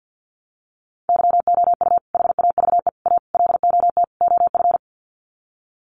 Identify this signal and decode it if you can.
Morse code, heard as a keyed tone, about 48 wpm, 700 Hz. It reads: YOU HAVE A LOT OF
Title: Sets of 5 Words Encore 50wpm